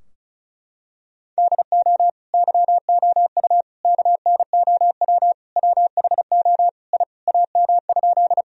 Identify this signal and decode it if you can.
Morse code, heard as a keyed tone, about 36 wpm, 700 Hz. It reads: DO YOU KNOW WHO I AM?
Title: Sentences from Top 1000 Words - Review Part 2 - 35wpm